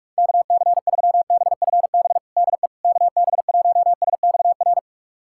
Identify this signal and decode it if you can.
Morse code, heard as a keyed tone, about 41 wpm, 690 Hz. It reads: KC3BFB DE KB1SXR
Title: USA Callsign DE Callsign - 6 Characters - 45wpm